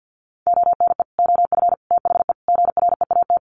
Morse code, heard as a keyed tone, about 50 wpm, 700 Hz. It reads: ONE OF THE GREAT
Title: Sets of 4 Words 50wpm